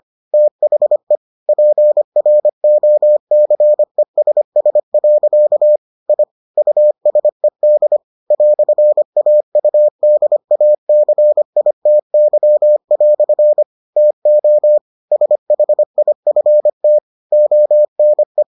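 Morse code, heard as a keyed tone, about 25 wpm, 600 Hz. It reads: THE PROCESS. I USED "AUDACITY" TO SHIFT ONE
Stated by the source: Morse edition of the Foundations of Amateur Radio episodes.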